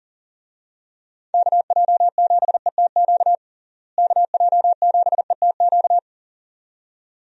Morse code, heard as a keyed tone, about 40 wpm, 700 Hz. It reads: KJ7ETQ KJ7ETQ
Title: USA Call Signs - 6 Characters - Repeated 2x - 40wpm